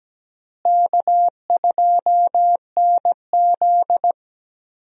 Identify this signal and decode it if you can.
Morse code, heard as a keyed tone, about 17 wpm, 700 Hz. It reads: K2NZ
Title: USA Call Signs - 4 Characters - Repeated 3x - 17wpm